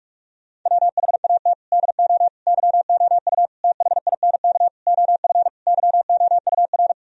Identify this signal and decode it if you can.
Morse code, heard as a keyed tone, about 46 wpm, 700 Hz. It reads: WHAT DO YOU THINK OF YOUR
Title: Sentences from Top 1000 Words - Review Part 4 - 45wpm